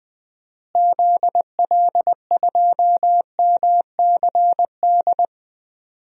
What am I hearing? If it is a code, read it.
Morse code, heard as a keyed tone, about 20 wpm, 700 Hz. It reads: ZL2MCD
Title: DXCC-100 Call Signs - 6 Characters - Repeated 2x - 20wpm